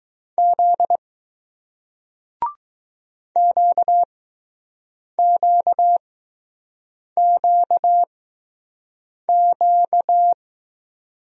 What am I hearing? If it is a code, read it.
Morse code, heard as a keyed tone, about 20 wpm, spaced as for 6 wpm, 700 Hz. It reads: Z QQQQ